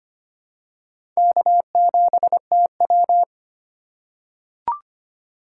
Morse code, heard as a keyed tone, about 25 wpm, 700 Hz. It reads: K7TW
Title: USA Call Signs - 4 Characters - 3x Character Spacing - 25wpm